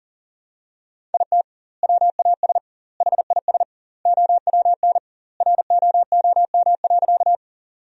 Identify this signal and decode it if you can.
Morse code, heard as a keyed tone, about 37 wpm, 700 Hz. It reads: IT WAS HIS OWN ROOM.
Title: 2x Word-Spacing - Sentences from Top 300 Words - 40wpm